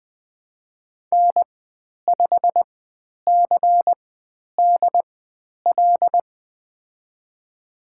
Morse code, heard as a keyed tone, about 20 wpm, spaced as for 10 wpm, 700 Hz. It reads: N5CDL
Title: 102 Lesson 17g - Letter u - Call Signs 20wpm